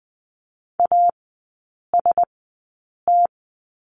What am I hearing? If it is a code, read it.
Morse code, heard as a keyed tone, about 20 wpm, spaced as for 8 wpm, 700 Hz. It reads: AST
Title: 041 Lesson 8e - Number 1 - Three Characters 20wpm